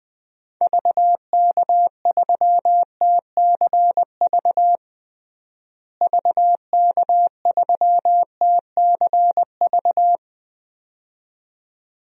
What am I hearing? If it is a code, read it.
Morse code, heard as a keyed tone, about 21 wpm, 700 Hz. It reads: VK3TCV VK3TCV
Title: DXCC-100 Call Signs - 6 Characters - Repeated 2x - 20wpm